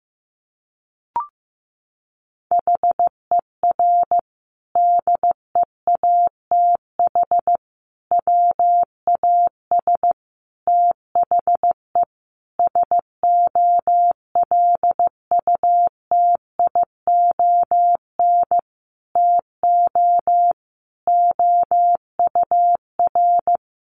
Morse code, heard as a keyed tone, about 15 wpm, 700 Hz. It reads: HER DEATH WAS THE SOLUTION TO OUR
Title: Sentences from Top 1300 Words 15wpm